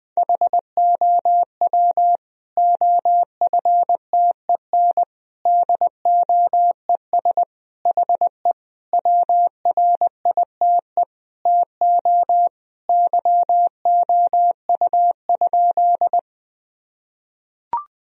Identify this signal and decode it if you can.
Morse code, heard as a keyed tone, about 20 wpm, 700 Hz. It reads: HOW OFTEN DOES HE WRITE TO YOU?
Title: Sentences from Top 1000 Words - Review Part 1 - 20wpm